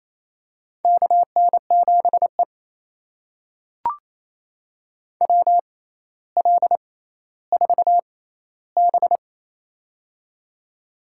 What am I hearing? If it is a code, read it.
Morse code, heard as a keyed tone, about 28 wpm, spaced as for 14 wpm, 700 Hz. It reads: KN7E WL4B
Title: USA Call Signs - 4 Characters - 6x Character Spacing - 28wpm